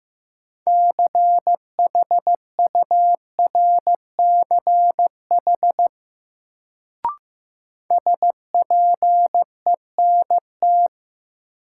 Morse code, heard as a keyed tone, about 15 wpm, 700 Hz. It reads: CHURCH SPENT
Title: Sets of 5 Words - New Word Drill - 15wpm